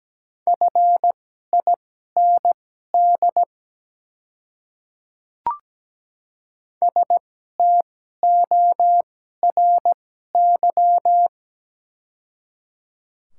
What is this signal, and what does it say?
Morse code, heard as a keyed tone, about 17 wpm, spaced as for 12 wpm, 700 Hz. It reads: FIND STORY
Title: 2x Character Spacing - Top 300 Words - 17wpm